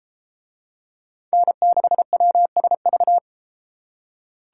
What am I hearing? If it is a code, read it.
Morse code, heard as a keyed tone, about 30 wpm, 700 Hz. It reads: N6WSV